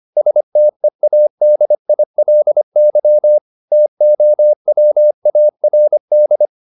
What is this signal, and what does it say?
Morse code, heard as a keyed tone, about 25 wpm, 600 Hz. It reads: STEADILY TOWARD
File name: the old man and the sea part 1 25wpm